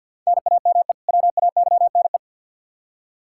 Morse code, heard as a keyed tone, about 51 wpm, 700 Hz. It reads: NAME WAYNE
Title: Practice QSO - Exchange 1 - Beginning - 50wpm